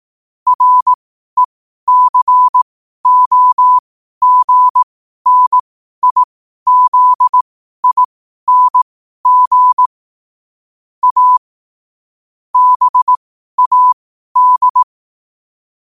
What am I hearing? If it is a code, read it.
Morse code, heard as a keyed tone, about 18 wpm, spaced as for 12 wpm, 1000 Hz. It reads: RECOGNIZING A BAD
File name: qotdfiles / 12wpm